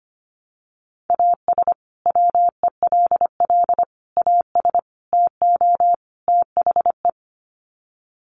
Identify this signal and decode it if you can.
Morse code, heard as a keyed tone, about 25 wpm, 700 Hz. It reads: AS WELL AS TO THE
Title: Sets of 5 Words Encore2 25wpm